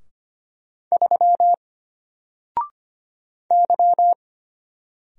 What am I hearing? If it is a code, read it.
Morse code, heard as a keyed tone, about 25 wpm, 700 Hz. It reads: 3 Y
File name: Single-Letter-Number-25wpm